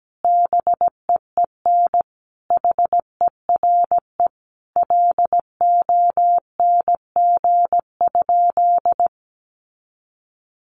Morse code, heard as a keyed tone, about 17 wpm, 700 Hz. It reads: BEEN HERE LONG?
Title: Sentences from Top 200 Words 17wpm